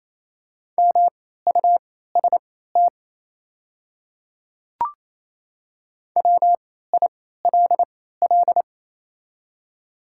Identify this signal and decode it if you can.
Morse code, heard as a keyed tone, about 28 wpm, spaced as for 16 wpm, 700 Hz. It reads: MUST WILL
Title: 3x Character Spacing - Top 200 Words - 28wpm